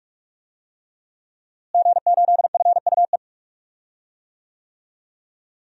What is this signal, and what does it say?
Morse code, heard as a keyed tone, about 45 wpm, 700 Hz. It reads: G8FUE
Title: DXCC-100 Call Signs - 5 Characters - Repeated 3x - 45wpm